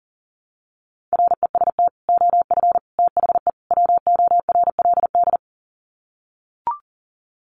Morse code, heard as a keyed tone, about 37 wpm, 700 Hz. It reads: REST OF THE WORLD
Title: Sets of 4 Words 40wpm